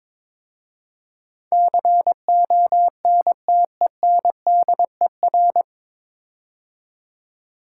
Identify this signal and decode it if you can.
Morse code, heard as a keyed tone, about 22 wpm, 700 Hz. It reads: CONTENDER
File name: 9 Letter Words - First 3 Letters Given - 22wpm